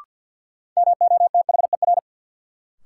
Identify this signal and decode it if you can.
Morse code, heard as a keyed tone, about 53 wpm, 700 Hz. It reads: MOTHER
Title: Sentences from Top 200 Words - New Word Drill - 50wpm